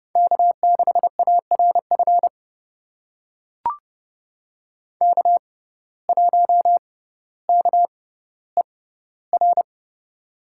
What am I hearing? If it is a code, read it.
Morse code, heard as a keyed tone, about 30 wpm, spaced as for 15 wpm, 700 Hz. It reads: K6ARF K1KER
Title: USA Call Signs - 5 Characters - 6x Character Spacing - 30wpm